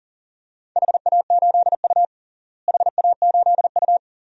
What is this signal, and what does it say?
Morse code, heard as a keyed tone, about 37 wpm, 700 Hz. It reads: HA8U HA8U
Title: DXCC-100 Call Signs - 4 Characters - Repeated 2x - 40wpm